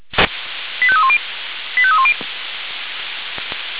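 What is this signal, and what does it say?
Five-tone selective call (ZVEI), heard as quick runs of five tones, tones of about 70 ms. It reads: 85310 85310
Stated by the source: ZVEI five-tone selective call received on SDR.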